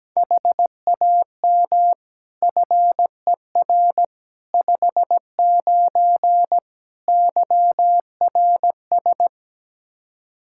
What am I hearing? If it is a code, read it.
Morse code, heard as a keyed tone, about 17 wpm, 700 Hz. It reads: HAM FER 59 YRS
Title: Practice QSO - Exchange 3 - More Info - 17wpm